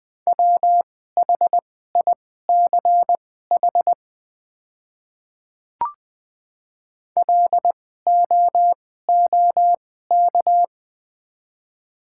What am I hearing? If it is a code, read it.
Morse code, heard as a keyed tone, about 20 wpm, spaced as for 15 wpm, 700 Hz. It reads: WHICH LOOK
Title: 2x Character Spacing - Top 200 Words - 20wpm